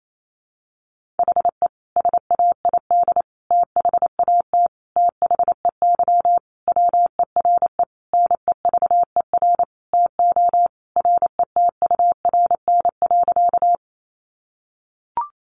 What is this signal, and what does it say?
Morse code, heard as a keyed tone, about 28 wpm, 700 Hz. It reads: HE SAID THAT THEY WERE NEVER TO RETURN.
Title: Sentences from Top 800 Words 28wpm